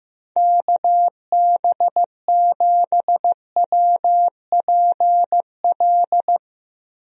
Morse code, heard as a keyed tone, about 15 wpm, 700 Hz. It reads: KB7WPL
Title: USA Call Signs - 6 Characters - Repeated 2x - 15wpm